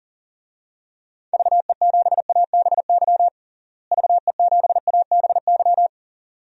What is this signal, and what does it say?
Morse code, heard as a keyed tone, about 37 wpm, 700 Hz. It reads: VE7ABY VE7ABY
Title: DXCC-100 Call Signs - 6 Characters - Repeated 2x - 40wpm